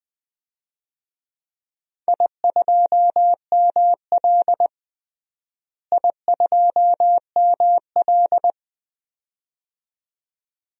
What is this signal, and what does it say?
Morse code, heard as a keyed tone, about 21 wpm, 700 Hz. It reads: I2ML I2ML
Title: DXCC-100 Call Signs - 4 Characters - Repeated 2x - 20wpm